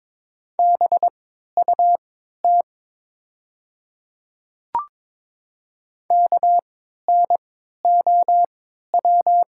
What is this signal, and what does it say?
Morse code, heard as a keyed tone, about 22 wpm, spaced as for 12 wpm, 700 Hz. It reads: BUT KNOW
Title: 3x Character Spacing - Top 300 Words - 22wpm